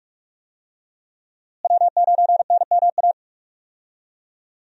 Morse code, heard as a keyed tone, about 46 wpm, 700 Hz. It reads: W9NMA
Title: USA Call Signs - 5 Characters - Repeated 2x - 45wpm